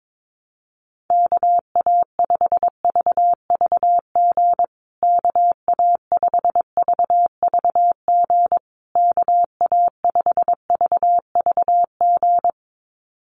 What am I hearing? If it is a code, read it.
Morse code, heard as a keyed tone, about 22 wpm, 700 Hz. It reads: KA5VVG KA5VVG KA5VVG
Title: USA Call Signs Repeated 3x 22wpm